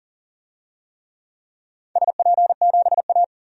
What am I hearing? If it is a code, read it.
Morse code, heard as a keyed tone, about 38 wpm, 700 Hz. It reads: SP7A
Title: DXCC-100 Call Signs - 4 Characters - 40wpm